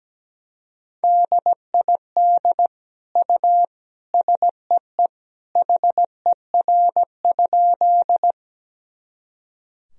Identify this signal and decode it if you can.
Morse code, heard as a keyed tone, about 17 wpm, 700 Hz. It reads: DID U SEE HER?
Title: Sentences from Top 100 Words and Abbreviations 17wpm